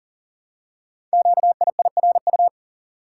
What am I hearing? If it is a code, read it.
Morse code, heard as a keyed tone, about 43 wpm, 700 Hz. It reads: QIIRU
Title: Code Groups - 5 Letters - 40wpm